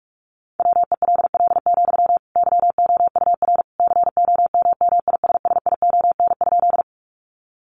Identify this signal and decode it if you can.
Morse code, heard as a keyed tone, about 44 wpm, 700 Hz. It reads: WELL, YOUR COMMISSION?
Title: Sentences from Top 1400 Words 45wpm